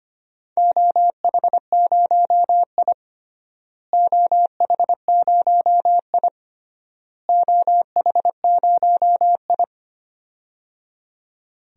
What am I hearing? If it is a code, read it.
Morse code, heard as a keyed tone, about 25 wpm, 700 Hz. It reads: OH0I OH0I OH0I
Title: DXCC-100 Call Signs - 4 Characters - Repeated 3x - 25wpm